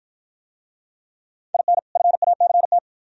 Morse code, heard as a keyed tone, about 53 wpm, 700 Hz. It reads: IN FACT